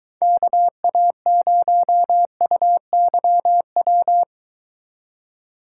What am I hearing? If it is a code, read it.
Morse code, heard as a keyed tone, about 24 wpm, 700 Hz. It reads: KA0UYW